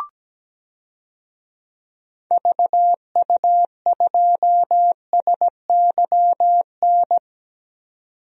Morse code, heard as a keyed tone, about 17 wpm, 700 Hz. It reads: VU2SYN
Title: DXCC-100 Call Signs - 6 Characters - 17wpm